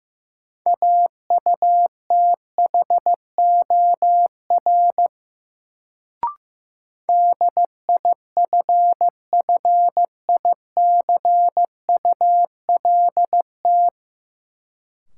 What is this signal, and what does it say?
Morse code, heard as a keyed tone, about 15 wpm, 700 Hz. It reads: AUTHOR DIFFICULT
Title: Sets of 4 Words - New Word Drill - 15wpm